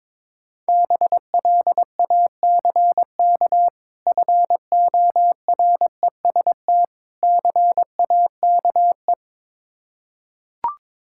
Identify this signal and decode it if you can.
Morse code, heard as a keyed tone, about 21 wpm, 690 Hz. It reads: BLACK FOREST CAKE
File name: Difficult Words with Hints - 22wpm